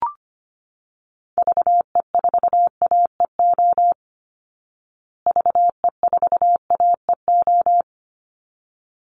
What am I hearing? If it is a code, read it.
Morse code, heard as a keyed tone, about 25 wpm, 700 Hz. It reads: VE4AEO VE4AEO
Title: Call Signs Repeated 3x - 25wpm